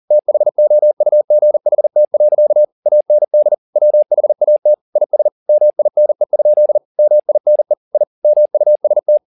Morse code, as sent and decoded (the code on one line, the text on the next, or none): - .... --- ..- --. .... - .-.-.- / .- -. -.. / .-- .... .- - / .. ... / -- .. -. ..--.. / -- .. -. / .. / -- ..- ... -
THOUGHT. AND WHAT IS MINE? MINE I MUST